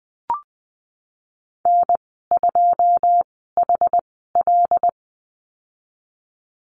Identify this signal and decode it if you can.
Morse code, heard as a keyed tone, about 20 wpm, spaced as for 14 wpm, 700 Hz. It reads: N2HL
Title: 088 Lesson 15g - Number 5 - Call Signs 20wpm